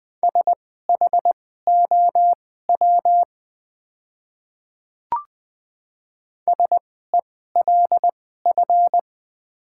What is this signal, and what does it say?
Morse code, heard as a keyed tone, about 20 wpm, spaced as for 14 wpm, 700 Hz. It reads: SHOW SELF
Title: 2x Character Spacing - Top 500 Words - 20wpm